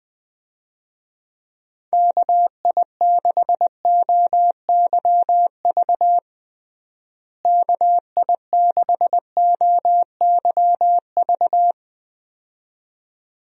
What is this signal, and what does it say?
Morse code, heard as a keyed tone, about 20 wpm, 700 Hz. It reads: KI6OYV KI6OYV
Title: USA Call Signs - 6 Characters - Repeated 2x - 20wpm